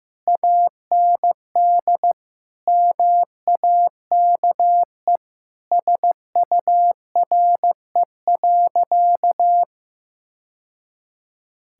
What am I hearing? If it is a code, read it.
Morse code, heard as a keyed tone, about 15 wpm, 700 Hz. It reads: AND MAKE SURE.
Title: Sentences from Top 1000 Words - Review Part 5 - 15wpm